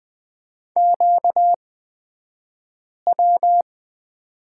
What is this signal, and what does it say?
Morse code, heard as a keyed tone, about 20 wpm, spaced as for 5 wpm, 700 Hz. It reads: QW
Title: 235 Lesson 38d - Letter q - Two Characters 20wpm